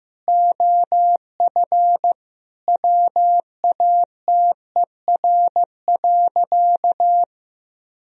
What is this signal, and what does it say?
Morse code, heard as a keyed tone, about 15 wpm, 700 Hz. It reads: OF WATER.
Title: Sentences from Top 1000 Words - Review Part 1 - 15wpm